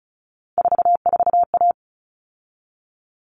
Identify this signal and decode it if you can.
Morse code, heard as a keyed tone, about 34 wpm, 700 Hz. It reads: 44A
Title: All Cut Numbers - 3-Digit 35wpm